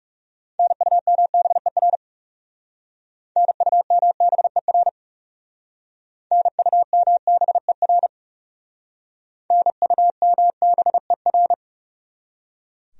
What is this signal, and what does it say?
Morse code, heard as a keyed tone, about 40 wpm, 700 Hz. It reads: NUMBER NUMBER NUMBER NUMBER